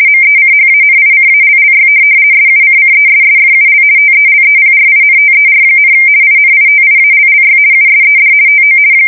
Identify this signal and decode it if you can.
RTTY, about 50 baud. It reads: MECHANICALLY SUCCESSFUL TELEPRINTER, INITIALLY USED TO
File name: fsk170-50B / teletype-corp